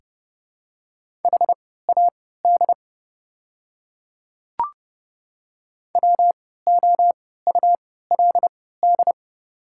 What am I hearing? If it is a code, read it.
Morse code, heard as a keyed tone, about 30 wpm, spaced as for 17 wpm, 690 Hz. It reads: HAD WOULD
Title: 3x Character Spacing - Top 300 Words - 30wpm